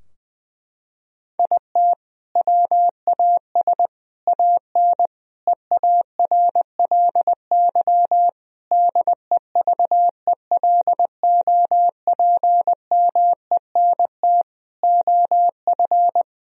Morse code, heard as a keyed tone, about 20 wpm, 700 Hz. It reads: IT WAS AN EARLY DEVELOPMENT OF
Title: Sentences from Top 1000 Words - Review Part 4 - 20wpm